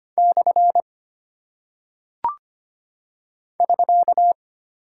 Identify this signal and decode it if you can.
Morse code, heard as a keyed tone, about 25 wpm, 700 Hz. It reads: / <SK>
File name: Single Punctuation-Prosign - 25wpm